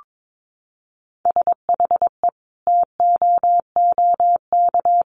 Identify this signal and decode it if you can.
Morse code, heard as a keyed tone, about 22 wpm, 700 Hz. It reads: SHE TOOK
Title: Sentences from Top 1300 Words 22wpm